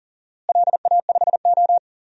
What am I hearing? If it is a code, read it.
Morse code, heard as a keyed tone, about 40 wpm, 700 Hz. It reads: LA5O
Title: DXCC-100 Call Signs - 4 Characters - Repeated 3x - 40wpm